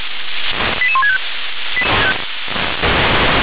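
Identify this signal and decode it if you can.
Five-tone selective call (ZVEI), heard as quick runs of five tones, tones of about 80 ms. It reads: 88165
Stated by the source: ZVEI five-tone selective call received by SDR.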